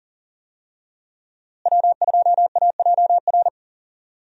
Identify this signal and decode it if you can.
Morse code, heard as a keyed tone, about 40 wpm, 700 Hz. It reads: W2AJR
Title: USA Call Signs - 5 Characters - Repeated 3x - 40wpm